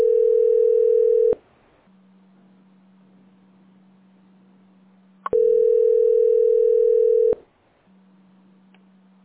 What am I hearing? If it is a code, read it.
Ringback tone (North America), 440 + 480 Hz.
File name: 17fc2dde-e349-411a-8a15